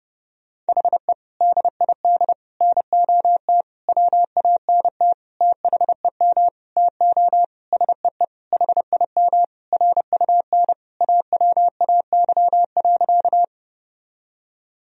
Morse code, heard as a keyed tone, about 31 wpm, 700 Hz. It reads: HE DID NOT WANT THEM TO SEE HIM RUN AWAY.
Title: Sentences from Top 1000 Words - Review Part 1 - 30wpm